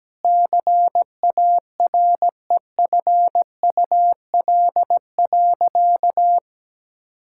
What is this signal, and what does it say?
Morse code, heard as a keyed tone, about 17 wpm, 700 Hz. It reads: CAREFUL.
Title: Sentences from Top 1600 Words 17wpm